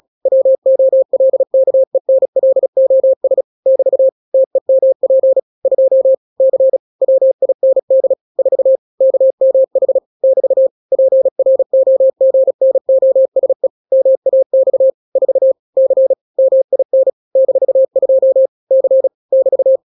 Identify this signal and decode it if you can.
Morse code, heard as a keyed tone, about 36 wpm, 530 Hz. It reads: WOLKENLOS <BT> TEMP 2 C WIND 4 KMH <BT> PROGNOSE MAX 4 C MIN -2 C <BT>
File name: daily / 35wpm